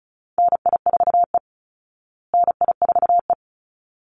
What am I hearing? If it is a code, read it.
Morse code, heard as a keyed tone, about 34 wpm, 700 Hz. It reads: NI4E NI4E
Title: Call Signs Repeated 3x - 35wpm